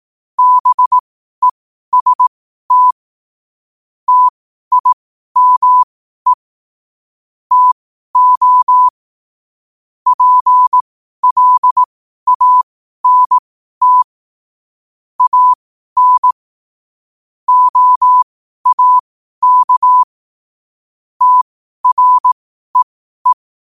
Morse code, heard as a keyed tone, about 18 wpm, spaced as for 12 wpm, 1000 Hz. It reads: BEST TIME TO PLANT AN OAK TREE
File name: qotdfiles / 12wpm